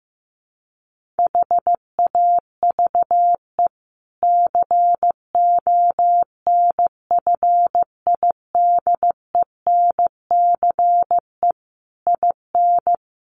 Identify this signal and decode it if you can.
Morse code, heard as a keyed tone, about 14 wpm, 700 Hz. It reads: HAVE CONFIDENCE IN
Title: Sentences from Top 1500 Words 15wpm